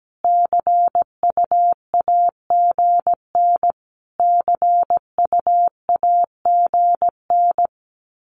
Morse code, heard as a keyed tone, about 17 wpm, 700 Hz. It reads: CUAGN CUAGN
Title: Abbreviations Repeated 3x - 17wpm